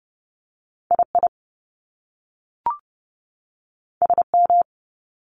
Morse code, heard as a keyed tone, about 30 wpm, spaced as for 3 wpm, 700 Hz. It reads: I IS M
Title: Top Two Letter Combinations in Words 30wpm